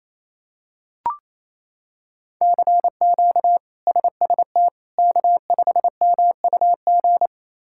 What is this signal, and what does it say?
Morse code, heard as a keyed tone, about 28 wpm, 700 Hz. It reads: CQ SST K5MUG
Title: K1USN SST QSO Practice - 28wpm